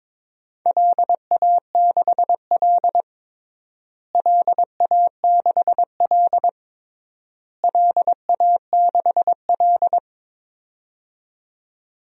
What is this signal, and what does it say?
Morse code, heard as a keyed tone, about 22 wpm, 700 Hz. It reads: LA6L LA6L LA6L
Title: DXCC-100 Call Signs - 4 Characters - Repeated 3x - 22wpm